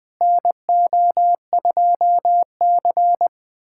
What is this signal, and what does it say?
Morse code, heard as a keyed tone, about 21 wpm, 700 Hz. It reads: NO2C
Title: USA Callsign DE Callsign - 4 Characters - 20wpm